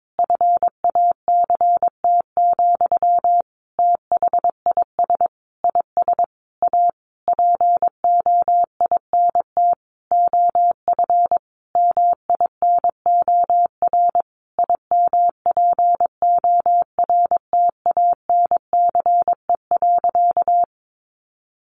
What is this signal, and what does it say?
Morse code, heard as a keyed tone, about 22 wpm, 700 Hz. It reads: FACT, THIS IS A POINT OF MINOR IMPORTANCE.
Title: Sentences from Top 1500 Words 22wpm